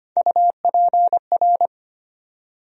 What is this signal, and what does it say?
Morse code, heard as a keyed tone, about 25 wpm, 700 Hz. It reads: UPR
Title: USA Call Sign Postfixes - 3 Letters 25wpm